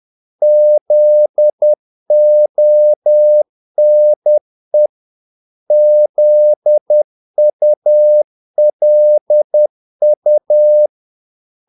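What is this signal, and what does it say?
Morse code, heard as a keyed tone, about 10 wpm, 600 Hz. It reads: ZONE ZULU
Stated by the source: Morse practice of short Danish words at 10 wpm, from CW med Gnister.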